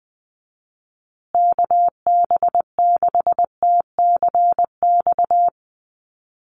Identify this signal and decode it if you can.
Morse code, heard as a keyed tone, about 20 wpm, 700 Hz. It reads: KB6TCX
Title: USA Call Signs Repeated 2x 20wpm